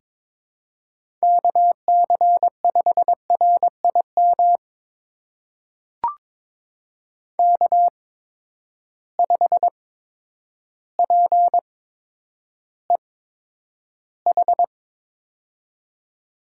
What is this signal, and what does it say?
Morse code, heard as a keyed tone, about 22 wpm, spaced as for 10 wpm, 700 Hz. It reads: KC5RIM K5PEH
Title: USA Call Signs - 8x Character Spacing - 22wpm